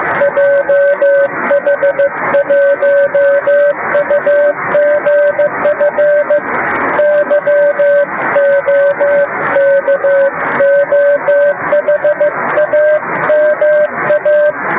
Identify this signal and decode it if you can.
Morse code, heard as a keyed tone, about 15 wpm, 560 Hz, with noise. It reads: JH1UGF YOKOHAMA
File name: Beacon / 10GbeaconJH1UGF awasaaki